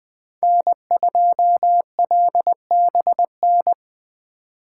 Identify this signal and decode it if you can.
Morse code, heard as a keyed tone, about 24 wpm, 700 Hz. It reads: N2LBN
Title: Speed Racing - USA 1x3 Call Signs 20wpm